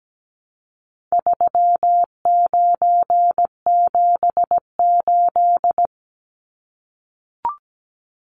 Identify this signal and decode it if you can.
Morse code, heard as a keyed tone, about 17 wpm, 700 Hz. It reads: 3978
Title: Numbers - 4-Digit 17wpm